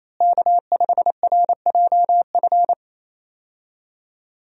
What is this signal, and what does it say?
Morse code, heard as a keyed tone, about 28 wpm, 700 Hz. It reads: K5RJF
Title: USA Call Signs - 5 Characters - Repeated 2x - 28wpm